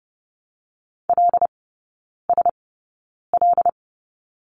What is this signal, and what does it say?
Morse code, heard as a keyed tone, about 30 wpm, spaced as for 9 wpm, 700 Hz. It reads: LSL
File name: course / 30:10wpm / 087 Lesson 15e - Number 5 - Three Characters 30wpm